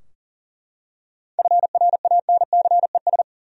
Morse code, heard as a keyed tone, about 39 wpm, 700 Hz. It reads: FRANCES
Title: USA Common Names - 40wpm